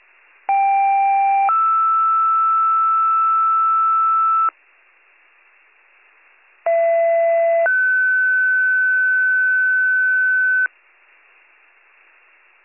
Two-tone page, heard as a one-second tone, then a three-second tone, about 790 then 1320 Hz, about 670 then 1500 Hz. It is heard from 0.5 to 10.7 s.